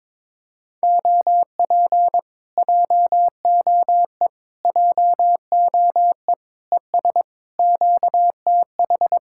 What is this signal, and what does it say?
Morse code, heard as a keyed tone, about 22 wpm, 700 Hz. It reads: OP JOE JOE ES QTH
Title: Speed-Racing Common QSO Phrases - 22wpm